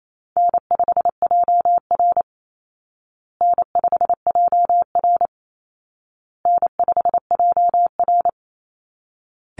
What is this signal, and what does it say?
Morse code, heard as a keyed tone, about 28 wpm, 700 Hz. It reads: N5JR N5JR N5JR
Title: Call Signs Repeated 2x - 28wpm